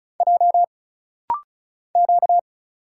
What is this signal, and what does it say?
Morse code, heard as a keyed tone, about 36 wpm, 700 Hz. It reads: J Q
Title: Single Letter-Number - Rapid-Fire 35wpm